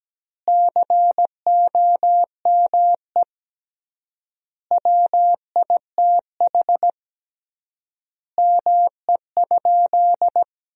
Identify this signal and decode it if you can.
Morse code, heard as a keyed tone, about 17 wpm, 700 Hz. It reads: COME WITH ME?
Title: 3x Word-Spacing - Sentences from Top 200 Words - 17wpm